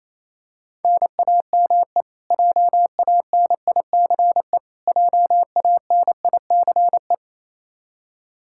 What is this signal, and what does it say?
Morse code, heard as a keyed tone, about 27 wpm, 700 Hz. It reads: NAME JANICE JANICE
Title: Practice QSO - Exchange 1 - Beginning - 28wpm